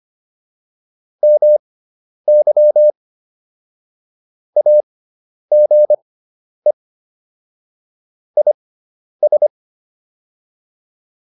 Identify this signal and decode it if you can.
Morse code, heard as a keyed tone, about 25 wpm, spaced as for 10 wpm, 600 Hz. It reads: MY AGE IS